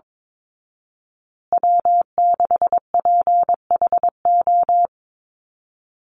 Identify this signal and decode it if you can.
Morse code, heard as a keyed tone, about 22 wpm, 700 Hz. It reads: W6PHO
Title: Call Signs Repeated 3x - 22wpm